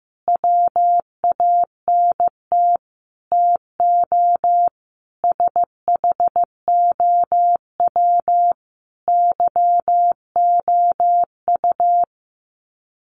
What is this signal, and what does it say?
Morse code, heard as a keyed tone, about 15 wpm, 700 Hz. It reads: WANT TO SHOW YOU
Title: Sets of 5 Words 15wpm